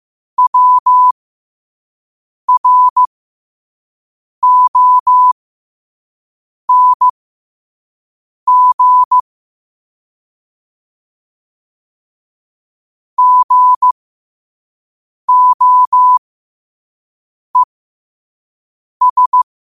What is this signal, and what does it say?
Morse code, heard as a keyed tone, about 15 wpm, spaced as for 5 wpm, 1000 Hz. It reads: WRONG GOES
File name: qotdfiles / 5wpm